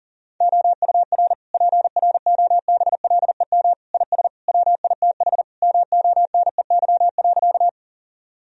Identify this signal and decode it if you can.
Morse code, heard as a keyed tone, about 40 wpm, 700 Hz. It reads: OUR PROBLEM IS WITH MONEY.
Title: Sentences from Top 1000 Words - Review Part 2 - 40wpm